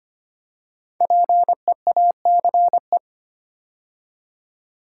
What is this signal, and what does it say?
Morse code, heard as a keyed tone, about 25 wpm, 700 Hz. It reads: PEACE
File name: 5 Letter Words - First 3 Letters Given - 25wpm